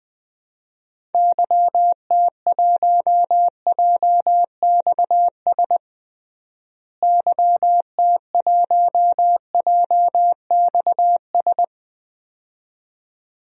Morse code, heard as a keyed tone, about 20 wpm, 700 Hz. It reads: YT1JXS YT1JXS
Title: DXCC-100 Call Signs - 6 Characters - Repeated 2x - 20wpm